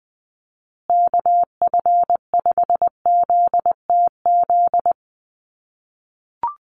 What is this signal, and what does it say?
Morse code, heard as a keyed tone, about 20 wpm, 700 Hz. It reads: KF5ZTZ
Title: USA Call Signs Repeated 3x 20wpm